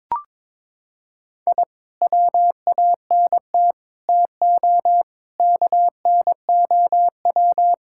Morse code, heard as a keyed tone, about 22 wpm, 700 Hz. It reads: I WANT TO KNOW
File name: practice / 22wpm / Sentences with 2-Word Phrases - 22wpm